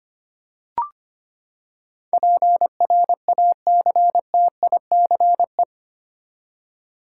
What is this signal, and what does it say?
Morse code, heard as a keyed tone, about 25 wpm, 700 Hz. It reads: PRACTICE
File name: Eight Letter Words - 25wpm